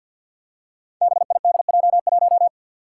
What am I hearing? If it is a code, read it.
Morse code, heard as a keyed tone, about 53 wpm, 700 Hz. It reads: BIDJ1
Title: Five Challenging Letter-Number Combinations 50wpm